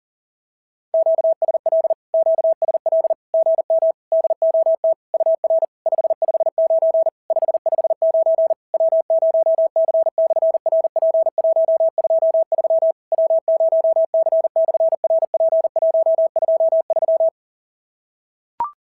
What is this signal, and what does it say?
Morse code, heard as a keyed tone, about 40 wpm, 650 Hz. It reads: QSL QSL GM DOT UR 559 559 W0C/RP123 W0C/RP123
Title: SOTA QSO Practice 40wpm